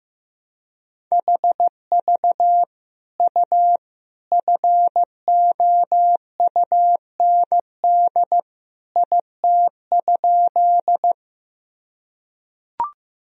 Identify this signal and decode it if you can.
Morse code, heard as a keyed tone, about 15 wpm, 700 Hz. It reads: HV U FOUND IT?
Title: Sentences from Top 300 Words and Abbreviations 15wpm